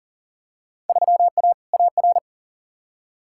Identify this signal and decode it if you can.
Morse code, heard as a keyed tone, about 40 wpm, 700 Hz. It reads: 3A AR
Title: Field Day Exchanges - 40wpm